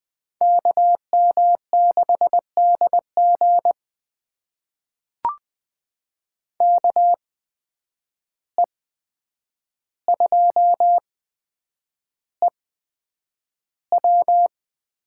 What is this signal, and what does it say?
Morse code, heard as a keyed tone, about 20 wpm, spaced as for 9 wpm, 700 Hz. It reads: KM6DG KE2EW
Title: USA Call Signs - 5 Characters - 8x Character Spacing - 20wpm